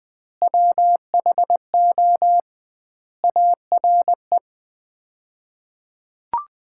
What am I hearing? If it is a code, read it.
Morse code, heard as a keyed tone, about 21 wpm, 700 Hz. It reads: WHO ARE
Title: 2x Word-Spacing - Sets of 3 Words - 20wpm